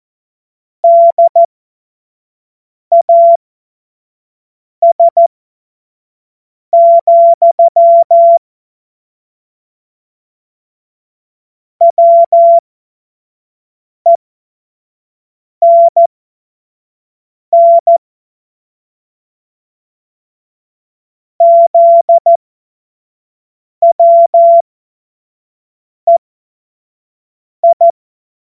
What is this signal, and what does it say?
Morse code, heard as a keyed tone, about 14 wpm, spaced as for 5 wpm, 680 Hz. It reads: DAS, WENN ZWEI
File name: Morsekurs / Witze / Witz 31. Mai 2022 25 F70